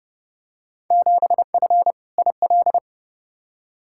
Morse code, heard as a keyed tone, about 30 wpm, 700 Hz. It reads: 7F IL
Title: Field Day Exchanges Encore - 30wpm